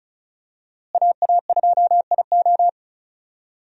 Morse code, heard as a keyed tone, about 36 wpm, 700 Hz. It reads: AA2IO
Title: USA Call Signs - 5 Characters - Repeated 3x - 35wpm